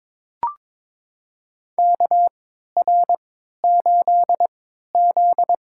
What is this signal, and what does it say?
Morse code, heard as a keyed tone, about 21 wpm, spaced as for 12 wpm, 700 Hz. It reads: KR8Z
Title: USA Call Signs - 3x Character Spacing - 22wpm